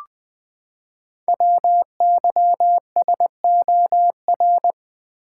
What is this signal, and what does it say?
Morse code, heard as a keyed tone, about 20 wpm, 700 Hz. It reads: WYSOR
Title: Code Groups - 5 Letters - 20wpm